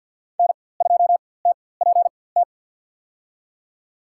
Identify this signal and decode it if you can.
Morse code, heard as a keyed tone, about 52 wpm, spaced as for 23 wpm, 700 Hz. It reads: N2TPT
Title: USA Call Signs - 5 Characters - 4x Character Spacing - 50wpm